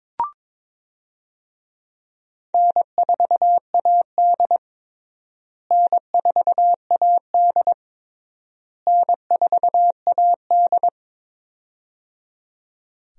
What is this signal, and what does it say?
Morse code, heard as a keyed tone, about 22 wpm, 700 Hz. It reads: N4AD N4AD N4AD
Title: USA Call Signs - 4 Characters - Repeated 3x - 22wpm